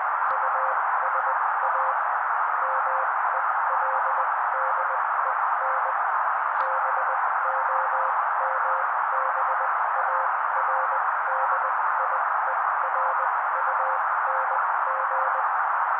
Morse code, heard as a keyed tone, about 20 wpm, spaced as for 16 wpm, 530 Hz. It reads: USA MELDEN BOMBARDIERUNG
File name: daily / 16wpm noise